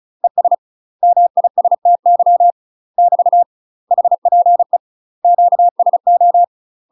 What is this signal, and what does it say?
Morse code, heard as a keyed tone, about 34 wpm, 700 Hz. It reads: ES MISTY <BT> HPE QSO
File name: UK MASTER QSOs TRAINING / 35WPM / UK MASTER QSOs 0003